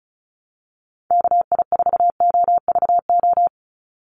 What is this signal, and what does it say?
Morse code, heard as a keyed tone, about 36 wpm, 700 Hz. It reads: KI4OVO
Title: USA Call Signs 35wpm